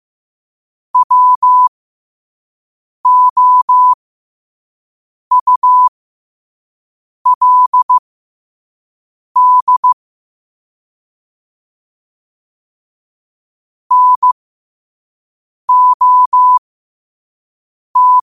Morse code, heard as a keyed tone, about 15 wpm, spaced as for 5 wpm, 1000 Hz. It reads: WOULD NOT
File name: qotdfiles / 5wpm